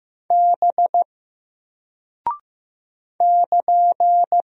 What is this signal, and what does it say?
Morse code, heard as a keyed tone, about 15 wpm, 700 Hz. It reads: B <KN>
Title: Single Character (Letter:Number:Punctuation:Prosign) Alt 15wpm